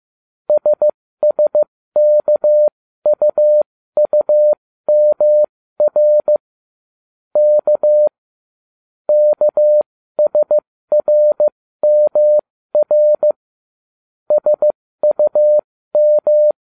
Morse code, heard as a keyed tone, about 15 wpm, spaced as for 13 wpm, 600 Hz. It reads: SSKUUMR K KSRMR SUM